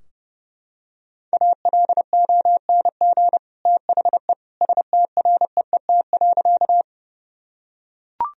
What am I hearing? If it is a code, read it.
Morse code, heard as a keyed tone, about 30 wpm, 700 Hz. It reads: ALONG THE STREET.
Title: Sentences from Top 1000 Words - Review Part 3 - 30wpm